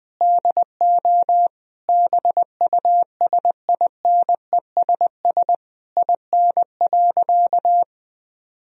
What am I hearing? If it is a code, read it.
Morse code, heard as a keyed tone, about 21 wpm, 700 Hz. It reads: DO BUSINESS IN.
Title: Sentences from Top 1000 Words - Review Part 3 - 20wpm